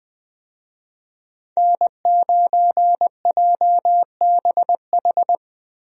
Morse code, heard as a keyed tone, about 20 wpm, 700 Hz. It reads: N9JBH
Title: USA Call Signs - 5 Characters - Repeated 2x - 20wpm